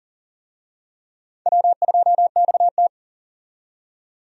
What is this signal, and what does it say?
Morse code, heard as a keyed tone, about 39 wpm, 700 Hz. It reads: W2XT
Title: USA Call Signs - 4 Characters - 40wpm